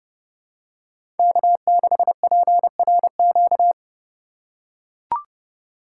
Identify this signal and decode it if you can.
Morse code, heard as a keyed tone, about 29 wpm, 690 Hz. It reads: K6PRQ
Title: USA Call Signs - 5 Characters - 2x Character Spacing - 30wpm